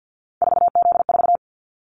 Morse code, heard as a keyed tone, about 50 wpm, 700 Hz. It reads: 4Z4
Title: Callsign Prefixes for Top 50 DXCC - 50wpm